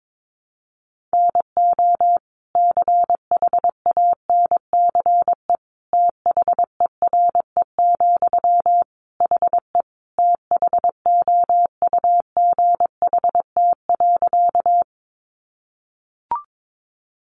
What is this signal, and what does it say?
Morse code, heard as a keyed tone, about 22 wpm, 700 Hz. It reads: NO CHANCE THERE, HE THOUGHT.
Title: Sentences from Top 800 Words 22wpm